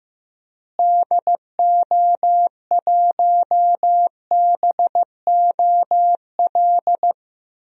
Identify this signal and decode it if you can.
Morse code, heard as a keyed tone, about 15 wpm, 700 Hz. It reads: DO1BOL
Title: DXCC-100 Call Signs - 6 Characters - Repeated 2x - 15wpm